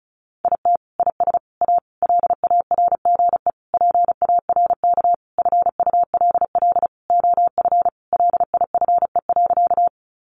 Morse code, heard as a keyed tone, about 34 wpm, 700 Hz. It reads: IT IS A LARGE PARK FULL OF LIFE.
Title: Sentences from Top 1300 Words 35wpm